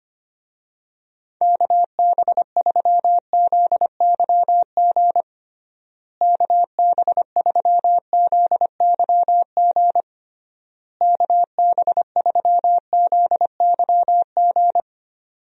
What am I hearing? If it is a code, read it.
Morse code, heard as a keyed tone, about 25 wpm, 700 Hz. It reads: KB3ZYG KB3ZYG KB3ZYG
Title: USA Call Signs - 6 Characters - Repeated 3x - 25wpm